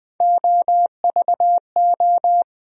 Morse code, heard as a keyed tone, about 20 wpm, 700 Hz. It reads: OVO
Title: USA Call Sign Postfixes - 3 Letters 20wpm